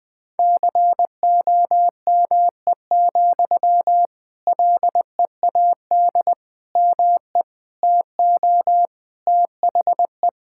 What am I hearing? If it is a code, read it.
Morse code, heard as a keyed tone, about 21 wpm, 700 Hz. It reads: COME, LEAD ME TO THE
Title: Sentences from Top 1600 Words 20wpm